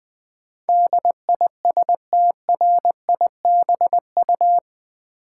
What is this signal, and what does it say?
Morse code, heard as a keyed tone, about 20 wpm, 700 Hz. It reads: DISTRIBU
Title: Build A Word - 4 Letters At A Time - 12-Letter Words - 20wpm